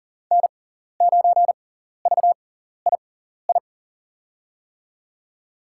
Morse code, heard as a keyed tone, about 40 wpm, spaced as for 14 wpm, 700 Hz. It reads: N9VII
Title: USA Call Signs - 5 Characters - 6x Character Spacing - 40wpm